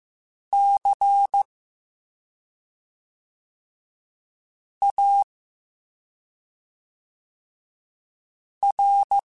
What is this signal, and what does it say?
Morse code, heard as a keyed tone, about 15 wpm, 800 Hz. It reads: C A R